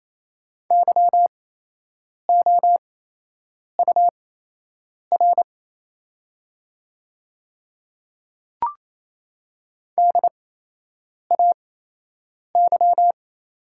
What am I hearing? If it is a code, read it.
Morse code, heard as a keyed tone, about 27 wpm, spaced as for 8 wpm, 700 Hz. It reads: YOUR DAY
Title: 8x Character Spacing - Top 100 Words - 28wpm